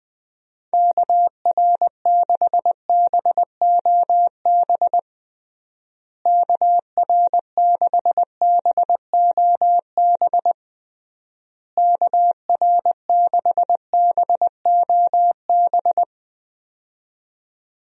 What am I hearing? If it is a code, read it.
Morse code, heard as a keyed tone, about 20 wpm, 700 Hz. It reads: KR6BOB KR6BOB KR6BOB
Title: USA Call Signs - 6 Characters - Repeated 3x - 20wpm